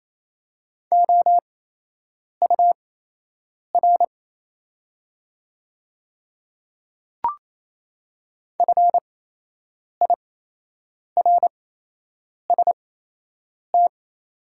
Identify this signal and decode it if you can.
Morse code, heard as a keyed tone, about 28 wpm, spaced as for 8 wpm, 700 Hz. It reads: OUR FIRST
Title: 8x Character Spacing - Top 300 Words - 28wpm